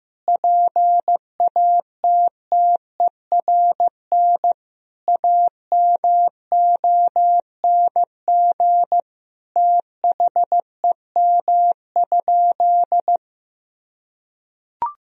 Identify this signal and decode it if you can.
Morse code, heard as a keyed tone, about 15 wpm, 700 Hz. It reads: PATTERN AMONG THEM?
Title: Sentences from Top 1000 Words - Review Part 2 - 15wpm